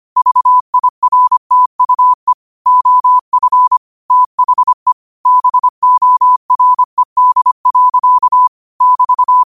Morse code, heard as a keyed tone, about 25 wpm, 1000 Hz. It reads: VIRTUE OF THE BORED. <BT>
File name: qotdfiles / 25wpm